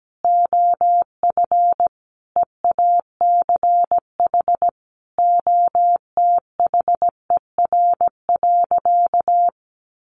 Morse code, heard as a keyed tone, about 17 wpm, 700 Hz. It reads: OF EACH OTHER.
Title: Sentences from Top 700 Words 17wpm